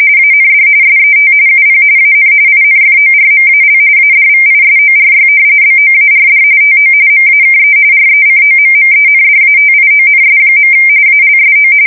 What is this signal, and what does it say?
RTTY, about 45.45 baud. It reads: AND SET UP AN INFORMAL ADVISORY BOARD TO MAKE RECOMMENDATIONS. ALSO
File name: fsk170 / usnavy3